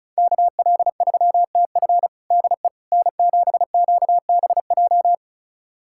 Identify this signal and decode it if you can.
Morse code, heard as a keyed tone, about 34 wpm, 690 Hz. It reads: KL3TF DE N7QBJ
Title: USA Callsign DE Callsign - 5 Characters - 35wpm